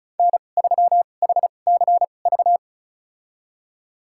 Morse code, heard as a keyed tone, about 34 wpm, spaced as for 25 wpm, 700 Hz. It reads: N3HCV
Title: USA Call Signs - 5 Characters - 2x Character Spacing - 35wpm